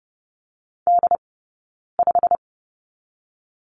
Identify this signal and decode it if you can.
Morse code, heard as a keyed tone, about 30 wpm, 700 Hz. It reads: D 5
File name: Lesson 15d - Number 5 - Two Characters 30wpm